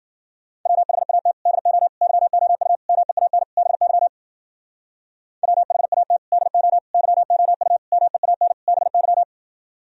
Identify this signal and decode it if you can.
Morse code, heard as a keyed tone, about 51 wpm, 700 Hz. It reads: WHAT DO YOU MEAN BY WHAT DO YOU MEAN BY